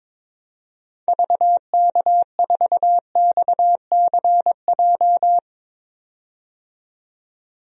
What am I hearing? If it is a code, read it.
Morse code, heard as a keyed tone, about 22 wpm, 700 Hz. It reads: VK4XCJ
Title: DXCC-100 Call Signs - 6 Characters - 22wpm